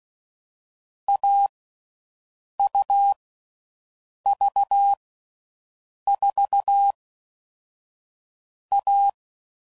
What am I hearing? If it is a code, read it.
Morse code, heard as a keyed tone, about 16 wpm, 800 Hz. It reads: A U V 4 A
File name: Difficult Combos